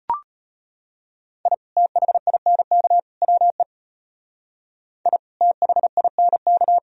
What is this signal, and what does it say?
Morse code, heard as a keyed tone, about 31 wpm, 700 Hz. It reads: I THINK WE I THINK
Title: Speed-Racing - Sets of 3 Words 25wpm